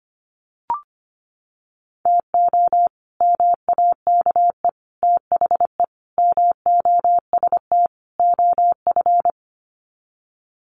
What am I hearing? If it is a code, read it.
Morse code, heard as a keyed tone, about 25 wpm, 700 Hz. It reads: TO MAKE THE MOST OF
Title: Sets of 5 Words Encore2 25wpm